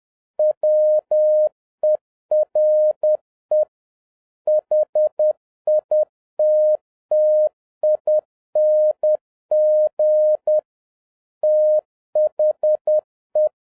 Morse code, heard as a keyed tone, about 10 wpm, 610 Hz. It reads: WERE HITTING THE
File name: 2011-07-mp3 / n31 THE 3.11 2 10wpm